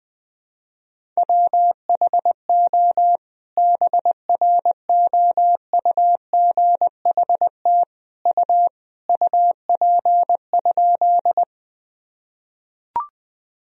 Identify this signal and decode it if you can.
Morse code, heard as a keyed tone, about 20 wpm, 700 Hz. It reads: WHO BROUGHT U UP?
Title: Sentences from Top 500 Words and Abbreviations 20wpm